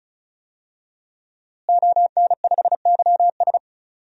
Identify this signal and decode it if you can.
Morse code, heard as a keyed tone, about 35 wpm, 700 Hz. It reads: ON5YS
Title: DXCC-100 Call Signs - 5 Characters - Repeated 3x - 35wpm